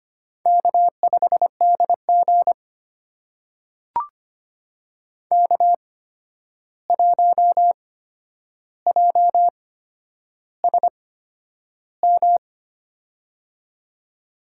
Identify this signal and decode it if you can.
Morse code, heard as a keyed tone, about 25 wpm, spaced as for 10 wpm, 700 Hz. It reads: K5DG K1JSM
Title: USA Call Signs - 8x Character Spacing - 25wpm